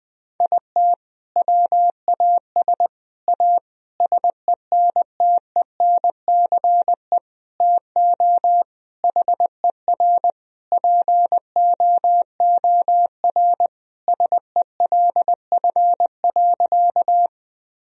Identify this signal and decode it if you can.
Morse code, heard as a keyed tone, about 21 wpm, 700 Hz. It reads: IT WAS A SENTENCE TO HER POOR SELF.
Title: Sentences from Top 1000 Words - Review Part 1 - 20wpm